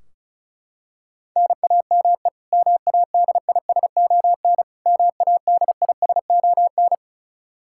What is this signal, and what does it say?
Morse code, heard as a keyed tone, about 34 wpm, 700 Hz. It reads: NAME MADISON MADISON
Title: K1USN SST QSO Practice 35wpm